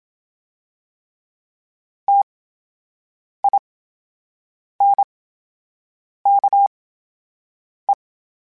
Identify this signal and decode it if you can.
Morse code, heard as a keyed tone, about 27 wpm, spaced as for 7 wpm, 800 Hz. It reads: TINKE
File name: Special INGs / Special INGs-mp3s / VerySlowToFaster Ing @25